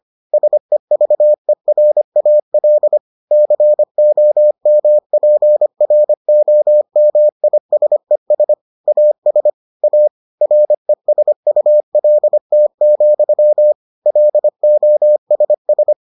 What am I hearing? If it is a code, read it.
Morse code, heard as a keyed tone, about 25 wpm, 600 Hz. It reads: SEVERAL COMPROMISES AS A RESULT, LOSS